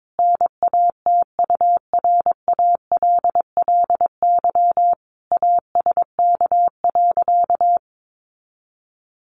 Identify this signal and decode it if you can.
Morse code, heard as a keyed tone, about 22 wpm, 700 Hz. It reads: NATURALLY ASK.
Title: Sentences from Top 1300 Words 22wpm